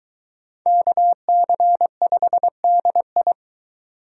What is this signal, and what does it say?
Morse code, heard as a keyed tone, about 21 wpm, 700 Hz. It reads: KC5DI